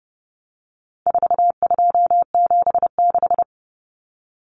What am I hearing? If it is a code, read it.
Morse code, heard as a keyed tone, about 30 wpm, 700 Hz. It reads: 4276
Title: Common Cut Numbers - 4-Digit 30wpm